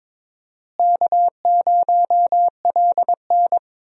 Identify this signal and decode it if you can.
Morse code, heard as a keyed tone, about 22 wpm, 700 Hz. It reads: K0LN
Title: USA Call Signs - 4 Characters - Repeated 3x - 22wpm